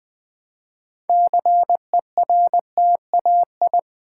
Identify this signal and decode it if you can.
Morse code, heard as a keyed tone, about 20 wpm, 700 Hz. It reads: CERTAI
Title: Build A Word - 3 Letters At A Time - 9-Letter Words - 20wpm